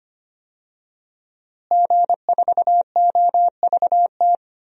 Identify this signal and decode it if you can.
Morse code, heard as a keyed tone, about 25 wpm, 700 Hz. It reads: G4OVT
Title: DXCC-100 Call Signs - 5 Characters - Repeated 3x - 25wpm